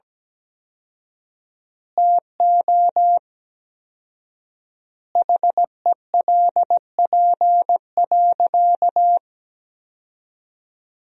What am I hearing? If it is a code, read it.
Morse code, heard as a keyed tone, about 17 wpm, 700 Hz. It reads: TO HELP.
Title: 4x Word-Spacing - Sentences from Top 300 Words - 17wpm